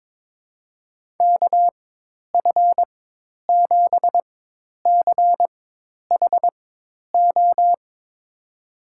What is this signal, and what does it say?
Morse code, heard as a keyed tone, about 22 wpm, spaced as for 10 wpm, 700 Hz. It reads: KF7CHO
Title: USA Call Signs - 4x Character Spacing - 22wpm